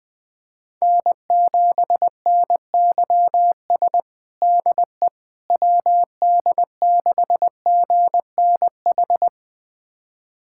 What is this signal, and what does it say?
Morse code, heard as a keyed tone, about 21 wpm, 700 Hz. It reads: N7NYS DE WD6GNH
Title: USA Callsign DE Callsign 20wpm